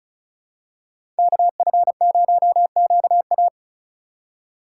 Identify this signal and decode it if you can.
Morse code, heard as a keyed tone, about 36 wpm, 700 Hz. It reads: KF0QA
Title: USA Call Signs - 5 Characters - 3x Character Spacing - 35wpm